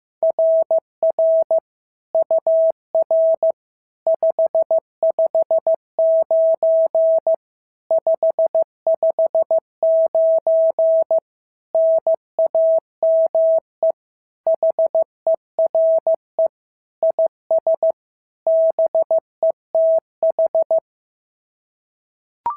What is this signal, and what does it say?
Morse code, heard as a keyed tone, about 15 wpm, 650 Hz. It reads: RR UR 559 559 NAME HERE IS BETH
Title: SOTA QSO Practice 15wpm